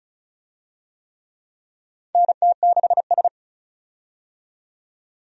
Morse code, heard as a keyed tone, about 36 wpm, 700 Hz. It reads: NT6S
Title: USA Call Signs - 4 Characters - 35wpm